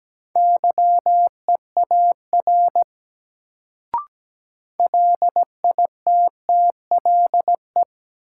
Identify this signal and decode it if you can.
Morse code, heard as a keyed tone, about 17 wpm, 700 Hz. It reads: YEAR LITTLE
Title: Sentences from Top 200 Words - New Word Drill - 17wpm